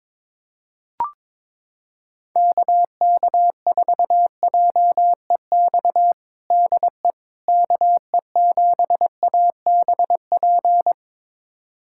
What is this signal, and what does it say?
Morse code, heard as a keyed tone, about 22 wpm, 700 Hz. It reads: KK4JEX DE KE7ABP
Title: USA Callsign DE Callsign - 6 Characters - 22wpm